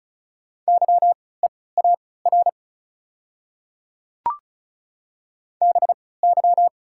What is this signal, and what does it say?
Morse code, heard as a keyed tone, about 36 wpm, spaced as for 20 wpm, 690 Hz. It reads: YEAR BY
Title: 3x Character Spacing - Top 300 Words - 35wpm